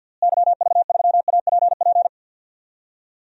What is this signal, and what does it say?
Morse code, heard as a keyed tone, about 51 wpm, 700 Hz. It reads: YV3APP
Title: DXCC-100 Call Signs - 6 Characters - Repeated 2x - 50wpm